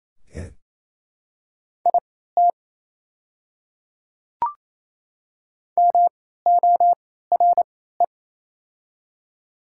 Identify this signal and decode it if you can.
Morse code, heard as a keyed tone, about 28 wpm, spaced as for 16 wpm, 700 Hz. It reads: IT MORE
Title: 3x Character Spacing - Top 100 Words - 28wpm